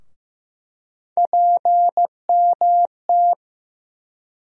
Morse code, heard as a keyed tone, about 15 wpm, 700 Hz. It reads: PMT
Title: USA Call Sign Postfixes - 3 Letters 15wpm